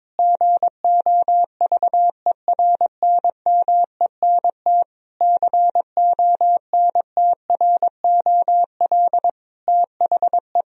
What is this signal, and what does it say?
Morse code, heard as a keyed tone, about 22 wpm, 700 Hz. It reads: GOVERNMENT CONTROL THE
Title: Sentences from Top 1000 Words - Review Part 3 - 22wpm